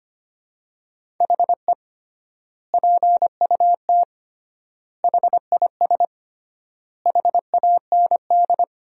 Morse code, heard as a keyed tone, about 25 wpm, 700 Hz. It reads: HE PUT HIS HAND
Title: 3x Word-Spacing - Sentences from Top 400 Words - 25wpm